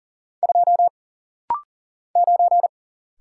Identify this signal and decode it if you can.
Morse code, heard as a keyed tone, about 40 wpm, 700 Hz. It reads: 2 9
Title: Single Letter-Number - Rapid-Fire 40wpm